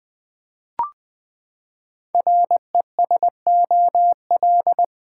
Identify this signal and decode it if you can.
Morse code, heard as a keyed tone, about 20 wpm, 700 Hz. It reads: RESOL
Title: Build A Word - 5 Letters At A Time - 10-Letter Words - 20wpm